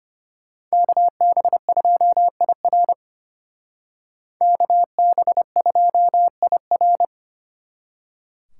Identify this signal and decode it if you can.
Morse code, heard as a keyed tone, about 32 wpm, 700 Hz. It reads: KB2IR KB2IR